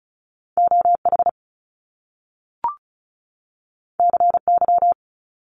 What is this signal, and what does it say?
Morse code, heard as a keyed tone, about 36 wpm, 700 Hz. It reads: OH CY
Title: Top Two Letter Combinations in Words 35wpm